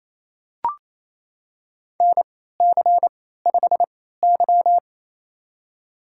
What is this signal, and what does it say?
Morse code, heard as a keyed tone, about 28 wpm, spaced as for 16 wpm, 700 Hz. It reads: NC5Y
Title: USA Call Signs - 4 Characters - 3x Character Spacing - 28wpm